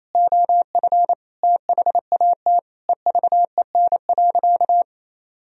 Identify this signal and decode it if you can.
Morse code, heard as a keyed tone, about 28 wpm, 700 Hz. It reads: OF THAT EVEN.
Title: Sentences from Top 1000 Words - Review Part 1 - 28wpm